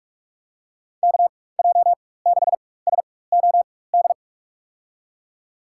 Morse code, heard as a keyed tone, about 42 wpm, spaced as for 20 wpm, 700 Hz. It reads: KJ6SOD
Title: USA Call Signs - 6 Characters - 4x Character Spacing - 45wpm